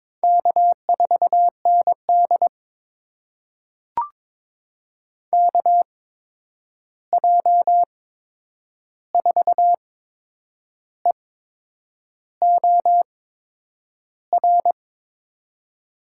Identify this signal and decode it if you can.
Morse code, heard as a keyed tone, about 22 wpm, spaced as for 8 wpm, 700 Hz. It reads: K4ND KJ4EOR
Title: USA Call Signs - 8x Character Spacing - 22wpm